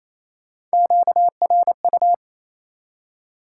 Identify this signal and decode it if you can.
Morse code, heard as a keyed tone, about 36 wpm, 700 Hz. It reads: QRU